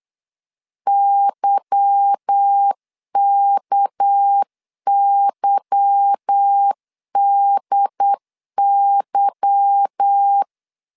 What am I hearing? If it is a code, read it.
Morse code, heard as a keyed tone, about 8 wpm, 800 Hz. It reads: YKYDY
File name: LEZIONI / 8 YJ45 / 8 Y 8wpm 3245